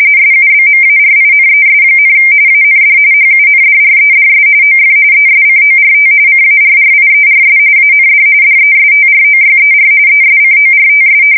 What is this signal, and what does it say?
RTTY, about 45.45 baud. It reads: SUBMARINES, AND HUNDREDS OF ESCORT CARRIERS AND DESTROYER ESCORTS
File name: fsk170 / usnavy4